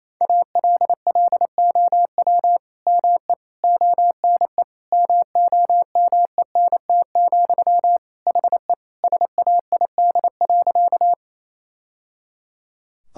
Morse code, heard as a keyed tone, about 28 wpm, 700 Hz. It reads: ALLOW ME ONE MOMENT, HE SAID.
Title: Sentences from Top 1000 Words - Review Part 5 - 28wpm